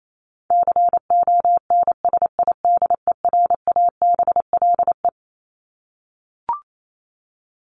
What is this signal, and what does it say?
Morse code, heard as a keyed tone, about 28 wpm, 700 Hz. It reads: CONSIDERABLE
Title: LY Suffix - Top 100 Words with Base Words 28wpm